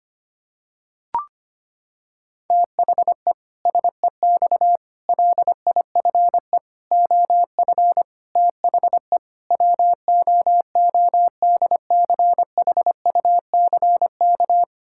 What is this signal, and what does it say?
Morse code, heard as a keyed tone, about 25 wpm, 700 Hz. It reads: THE SEX LIFE OF THE WOODCHUCK
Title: Pangram Challenge v2 - 25wpm